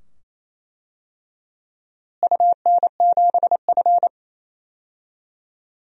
Morse code, heard as a keyed tone, about 27 wpm, 700 Hz. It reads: UN7F
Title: DXCC-100 Call Signs - 4 Characters - Repeated 2x - 28wpm